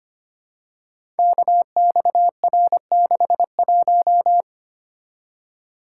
Morse code, heard as a keyed tone, about 25 wpm, 700 Hz. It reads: KXR61
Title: Code Groups - 5 Letter:Numbers - 25wpm